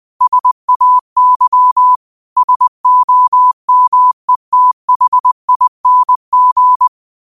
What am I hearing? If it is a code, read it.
Morse code, heard as a keyed tone, about 20 wpm, 1000 Hz. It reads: SAY SOMETHING
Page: Quotes for Fri, 15 Aug 2025 in Morse Code at 20 words per minute.